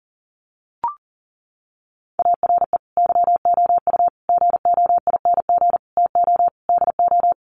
Morse code, heard as a keyed tone, about 40 wpm, 700 Hz. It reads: ARE YOU GOING TO DO
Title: Sets of 5 Words 40wpm